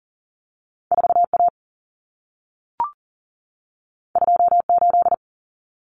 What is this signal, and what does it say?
Morse code, heard as a keyed tone, about 40 wpm, 700 Hz. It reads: 4A 28
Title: Common Cut Numbers - 2-Digit 40wpm